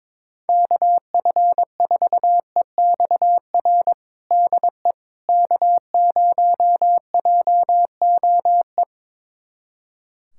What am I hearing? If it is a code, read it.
Morse code, heard as a keyed tone, about 22 wpm, 700 Hz. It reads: KF4EXR DE K0JOE
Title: USA Callsign DE Callsign 22wpm